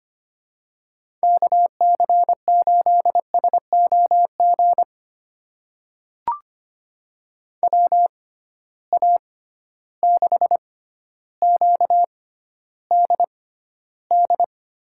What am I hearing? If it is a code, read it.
Morse code, heard as a keyed tone, about 25 wpm, spaced as for 13 wpm, 700 Hz. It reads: KC8SOG WA6QDD
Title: USA Call Signs - 6 Characters - 6x Character Spacing - 25wpm